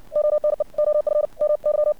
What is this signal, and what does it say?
Morse code, heard as a keyed tone, about 56 wpm, 600 Hz. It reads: ONE OF MY
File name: archived / CRSnet